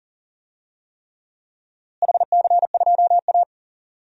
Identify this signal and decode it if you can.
Morse code, heard as a keyed tone, about 40 wpm, 700 Hz. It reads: HC2A
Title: DXCC-100 Call Signs - 4 Characters - Repeated 3x - 40wpm